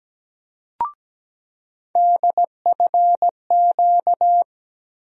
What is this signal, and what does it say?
Morse code, heard as a keyed tone, about 17 wpm, 700 Hz. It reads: DFQ
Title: USA Call Sign Postfixes - 3 Letters 17wpm